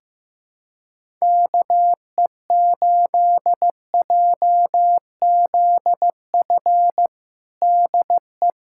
Morse code, heard as a keyed tone, about 15 wpm, 700 Hz. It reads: KE8JZF DE
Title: USA Callsign DE Callsign - 6 Characters - 15wpm